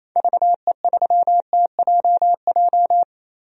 Morse code, heard as a keyed tone, about 28 wpm, 700 Hz. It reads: VE3TJJ
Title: DXCC-100 Call Signs - 6 Characters - Repeated 3x - 28wpm